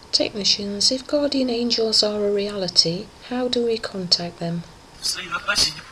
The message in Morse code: none